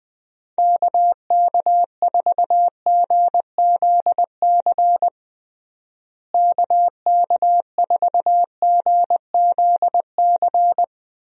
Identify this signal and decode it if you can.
Morse code, heard as a keyed tone, about 21 wpm, 700 Hz. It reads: KK4GZC KK4GZC
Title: USA Call Signs - 6 Characters - Repeated 2x - 20wpm